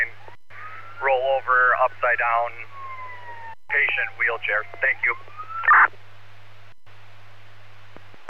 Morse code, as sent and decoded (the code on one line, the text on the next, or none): none